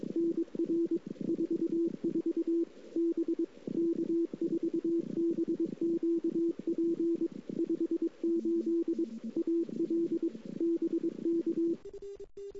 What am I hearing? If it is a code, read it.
Morse code, heard as a keyed tone, about 22 wpm, 330 Hz. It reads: NR 44 B K4BQP 58 AL BK
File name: CW / 2008-ss-cw-k5tr-op-n5rz -00:00